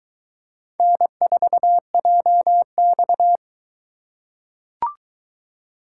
Morse code, heard as a keyed tone, about 22 wpm, 700 Hz. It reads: N4JX